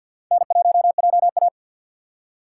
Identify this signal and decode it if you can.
Morse code, heard as a keyed tone, about 48 wpm, 700 Hz. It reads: N1JA
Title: USA Call Signs - 3x Character Spacing - 50wpm